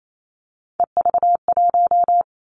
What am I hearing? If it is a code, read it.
Morse code, heard as a keyed tone, about 28 wpm, 700 Hz. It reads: EV1
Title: Three Challenging Letter-Number Combinations 28wpm